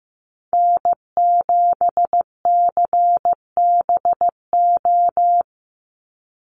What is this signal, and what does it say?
Morse code, heard as a keyed tone, about 15 wpm, 700 Hz. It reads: N7CBO
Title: USA Call Signs 15wpm